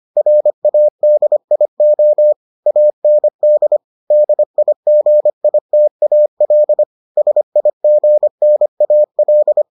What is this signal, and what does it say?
Morse code, heard as a keyed tone, about 25 wpm, 600 Hz. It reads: RADIO AND DIGITAL SIGNAL